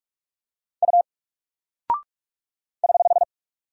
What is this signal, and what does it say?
Morse code, heard as a keyed tone, about 43 wpm, 700 Hz. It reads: U <HH>
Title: Single Character (Letter:Number:Punctuation:Prosign) Alt 45wpm